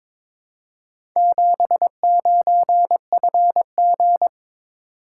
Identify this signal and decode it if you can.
Morse code, heard as a keyed tone, about 22 wpm, 690 Hz. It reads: 79FG
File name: practice / 22wpm / Four Letter-Number Combinations - 22wpm